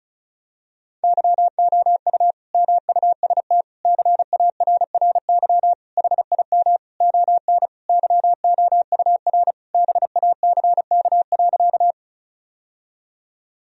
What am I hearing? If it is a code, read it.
Morse code, heard as a keyed tone, about 34 wpm, 700 Hz. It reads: YOU MUST CARRY HIM ON YOUR BACK.
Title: Sentences from Top 1000 Words - Review Part 1 - 35wpm